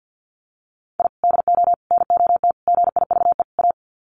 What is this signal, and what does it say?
Morse code, heard as a keyed tone, about 49 wpm, 700 Hz. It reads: I DO NOT GIVE A
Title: Sets of 5 Words Encore2 50wpm